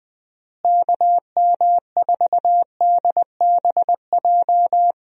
Morse code, heard as a keyed tone, about 20 wpm, 700 Hz. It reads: KM4DBJ
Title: USA Call Signs - 6 Characters - Repeated 2x - 20wpm